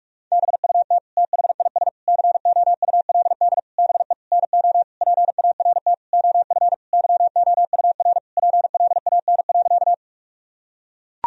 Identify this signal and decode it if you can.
Morse code, heard as a keyed tone, about 41 wpm, 700 Hz. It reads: BUT THIS COULD BE NO PART OF YOUR PLAN.
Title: Sentences from Top 1000 Words - Review Part 2 - 45wpm